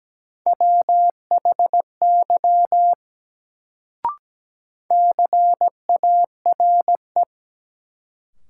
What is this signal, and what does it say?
Morse code, heard as a keyed tone, about 17 wpm, 700 Hz. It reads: WHY CARE
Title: Sets of 3 Words - New Word Drill - 17wpm